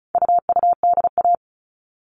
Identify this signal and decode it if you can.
Morse code, heard as a keyed tone, about 34 wpm, 700 Hz. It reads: UUDA
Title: All Cut Numbers - 4-Digit 35wpm